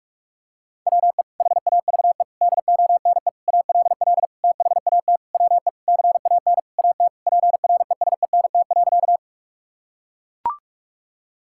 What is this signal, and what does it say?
Morse code, heard as a keyed tone, about 41 wpm, 700 Hz. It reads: WE HAVE DONE ALL THAT WE CAN AT PRESENT.
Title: Sentences from Top 1000 Words - Review Part 5 - 45wpm